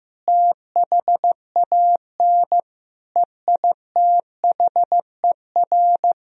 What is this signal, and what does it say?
Morse code, heard as a keyed tone, about 14 wpm, 700 Hz. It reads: THAN EITHER
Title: Sentences from Top 1000 Words - Review Part 4 - 15wpm